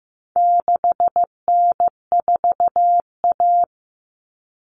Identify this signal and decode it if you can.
Morse code, heard as a keyed tone, about 15 wpm, 700 Hz. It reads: 6N4A
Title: Common Cut Numbers - 4-Digit 15wpm